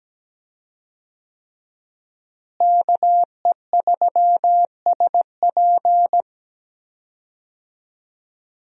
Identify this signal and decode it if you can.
Morse code, heard as a keyed tone, about 17 wpm, 700 Hz. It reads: KE3SP
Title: USA Call Signs - 5 Characters - Repeated 3x - 17wpm